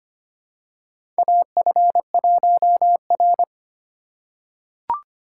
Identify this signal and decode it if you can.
Morse code, heard as a keyed tone, about 25 wpm, 690 Hz. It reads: AF1R
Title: USA Call Signs - 4x Character Spacing - 25wpm